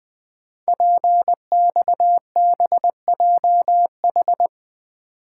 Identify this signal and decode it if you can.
Morse code, heard as a keyed tone, about 20 wpm, 700 Hz. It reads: PXBJH
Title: Code Groups - 5 Letters - 20wpm